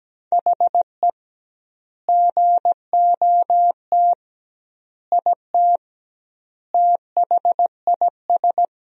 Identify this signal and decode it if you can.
Morse code, heard as a keyed tone, about 17 wpm, 700 Hz. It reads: HE GOT IT THIS
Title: 2x Word-Spacing - Sentences from Top 400 Words - 17wpm